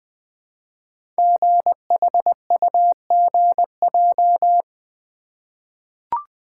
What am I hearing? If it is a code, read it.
Morse code, heard as a keyed tone, about 20 wpm, 700 Hz. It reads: GHUGJ
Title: Code Groups - 5 Letters - 20wpm